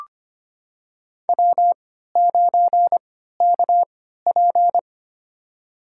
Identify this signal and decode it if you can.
Morse code, heard as a keyed tone, about 25 wpm, spaced as for 14 wpm, 700 Hz. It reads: W9KP
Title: USA Call Signs - 3x Character Spacing - 25wpm